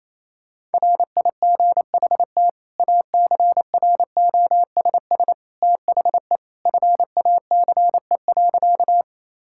Morse code, heard as a keyed tone, about 28 wpm, 700 Hz. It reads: RIGHT ACROSS THE FACE.
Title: Sentences from Top 1000 Words - Review Part 5 - 28wpm